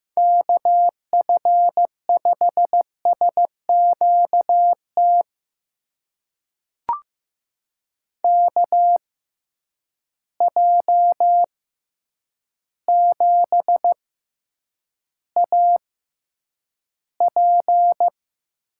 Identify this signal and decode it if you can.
Morse code, heard as a keyed tone, about 15 wpm, spaced as for 8 wpm, 700 Hz. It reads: KF5SQT KJ7AP
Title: USA Call Signs - 6 Characters - 6x Character Spacing - 15wpm